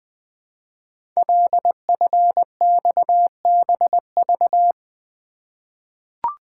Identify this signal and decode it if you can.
Morse code, heard as a keyed tone, about 20 wpm, 700 Hz. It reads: LFXBV
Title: Code Groups - 5 Letters - 20wpm